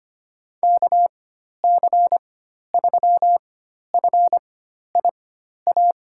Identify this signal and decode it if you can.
Morse code, heard as a keyed tone, about 25 wpm, spaced as for 12 wpm, 700 Hz. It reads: KC3FIA
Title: USA Call Signs - 4x Character Spacing - 25wpm